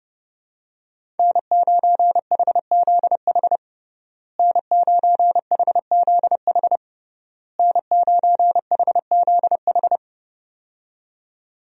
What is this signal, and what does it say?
Morse code, heard as a keyed tone, about 31 wpm, 700 Hz. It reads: N9HZH N9HZH N9HZH
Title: USA Call Signs - 5 Characters - Repeated 3x - 30wpm